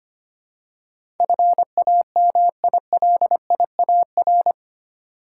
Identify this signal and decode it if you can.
Morse code, heard as a keyed tone, about 25 wpm, 700 Hz. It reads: FAMILIAR
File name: 8 Letter Words - First 3 Letters Given - 25wpm